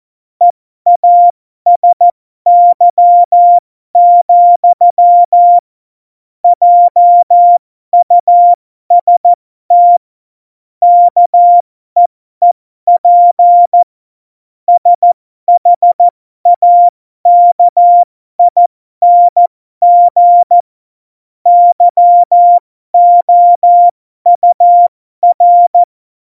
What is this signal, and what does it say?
Morse code, heard as a keyed tone, about 14 wpm, spaced as for 12 wpm, 700 Hz. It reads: EASY, JUST KEEP SHAKING YOUR
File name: Morsekurs / Witze